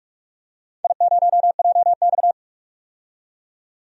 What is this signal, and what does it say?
Morse code, heard as a keyed tone, about 43 wpm, 700 Hz. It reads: I0JX
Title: DXCC-100 Call Signs - 4 Characters - 45wpm